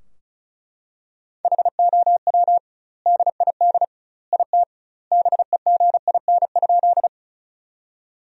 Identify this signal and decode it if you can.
Morse code, heard as a keyed tone, about 34 wpm, 700 Hz. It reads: HOW DID IT BEGIN?
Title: 2x Word-Spacing - Sentences from Top 300 Words - 35wpm